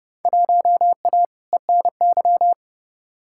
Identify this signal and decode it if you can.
Morse code, heard as a keyed tone, about 30 wpm, 700 Hz. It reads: 1A ENY
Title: Field Day Exchanges Encore - 30wpm